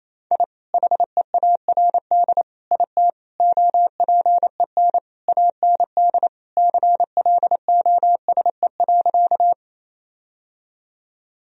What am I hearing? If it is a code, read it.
Morse code, heard as a keyed tone, about 28 wpm, 700 Hz. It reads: I HEARD IT OPEN AND CLOSE.
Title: Sentences from Top 1000 Words - Review Part 1 - 28wpm